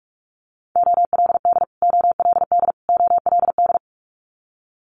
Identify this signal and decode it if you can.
Morse code, heard as a keyed tone, about 46 wpm, 690 Hz. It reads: OLD OLD OLD
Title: Multi-letter Phonemes Repeated 3x 45wpm